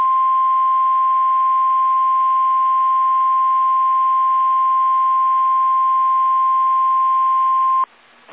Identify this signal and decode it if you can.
Emergency Alert System (SAME) tones (the 1050 Hz attention tone). It is heard from 0.9 to 7.8 s.